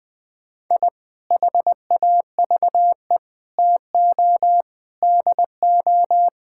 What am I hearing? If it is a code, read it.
Morse code, heard as a keyed tone, about 20 wpm, 700 Hz. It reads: I HAVE TO DO
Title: Sentences with 3-Word Phrases - 20wpm